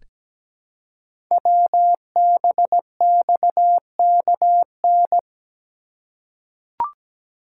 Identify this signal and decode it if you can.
Morse code, heard as a keyed tone, about 17 wpm, 700 Hz. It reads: WBXKN
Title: Code Groups - 5 Letters - 17wpm